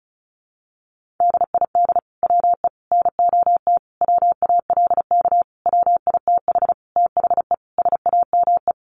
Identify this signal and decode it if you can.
Morse code, heard as a keyed tone, about 34 wpm, 700 Hz. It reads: DID WE NOT WALK WITH THE SAME
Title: Sentences from Top 1400 Words 35wpm